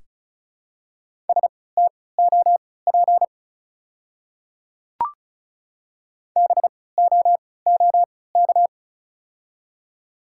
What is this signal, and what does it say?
Morse code, heard as a keyed tone, about 36 wpm, spaced as for 20 wpm, 700 Hz. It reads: STOP BOOK
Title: 3x Character Spacing - Top 300 Words - 35wpm